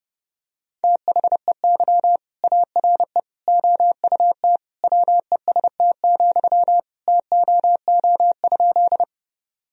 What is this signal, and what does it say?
Morse code, heard as a keyed tone, about 32 wpm, 700 Hz. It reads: THEY ARE OUT WEST, TOO?
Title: Sentences from Top 1000 Words - Review Part 2 - 30wpm